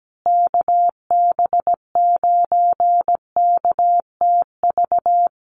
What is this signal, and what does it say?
Morse code, heard as a keyed tone, about 17 wpm, 700 Hz. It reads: KB9KTV
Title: USA Call Signs Repeated 2x 17wpm